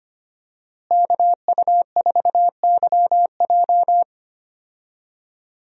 Morse code, heard as a keyed tone, about 25 wpm, 700 Hz. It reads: KU4YJ
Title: USA Call Signs - 5 Characters - Repeated 2x - 25wpm